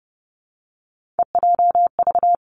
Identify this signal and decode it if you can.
Morse code, heard as a keyed tone, about 30 wpm, 700 Hz. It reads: EJV
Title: Three Challenging Letter-Number Combinations 30wpm